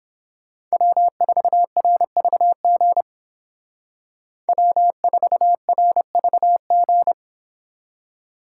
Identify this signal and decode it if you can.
Morse code, heard as a keyed tone, about 32 wpm, 700 Hz. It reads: W4RVG W4RVG